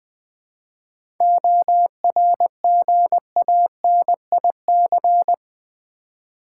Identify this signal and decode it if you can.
Morse code, heard as a keyed tone, about 20 wpm, 700 Hz. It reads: ORGANIC
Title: 7 Letter Words - First 3 Letters Given - 20wpm